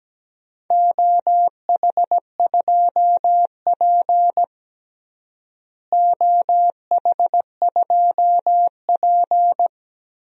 Morse code, heard as a keyed tone, about 17 wpm, 700 Hz. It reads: OH2P OH2P
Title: DXCC-100 Call Signs - 4 Characters - Repeated 3x - 17wpm